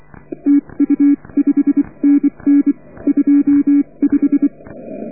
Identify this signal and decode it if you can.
Morse code, heard as a keyed tone, about 24 wpm, spaced as for 21 wpm, 280 Hz. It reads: TU5NN 25